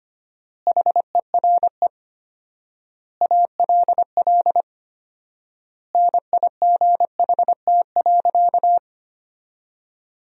Morse code, heard as a keyed tone, about 25 wpm, 700 Hz. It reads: HERE ALL NIGHT.
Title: 4x Word-Spacing - Sentences from Top 300 Words - 25wpm